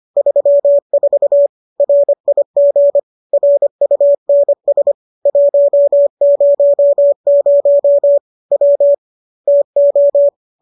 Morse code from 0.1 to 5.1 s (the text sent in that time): 34 RIG RUNS